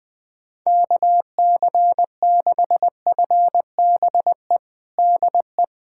Morse code, heard as a keyed tone, about 21 wpm, 700 Hz. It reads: KC6FBE DE
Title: USA Callsign DE Callsign - 6 Characters - 20wpm